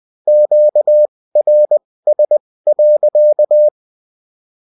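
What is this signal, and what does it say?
Morse code, heard as a keyed tone, about 20 wpm, spaced as for 16 wpm, 600 Hz. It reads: QRS.
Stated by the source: Morse code bulletin of a French amateur CW union, sent as text and in CW.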